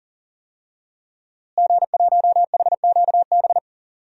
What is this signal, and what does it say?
Morse code, heard as a keyed tone, about 40 wpm, 700 Hz. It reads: G1HQB
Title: DXCC-100 Call Signs - 5 Characters - Repeated 3x - 40wpm